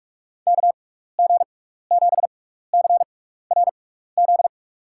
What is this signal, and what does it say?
Morse code, heard as a keyed tone, about 44 wpm, spaced as for 15 wpm, 700 Hz. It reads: KG7CRZ
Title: USA Call Signs - 6x Character Spacing - 45wpm